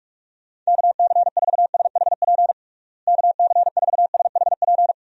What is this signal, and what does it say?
Morse code, heard as a keyed tone, about 46 wpm, 700 Hz. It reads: KC4SHP KC4SHP
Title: USA Call Signs - 6 Characters - Repeated 3x - 45wpm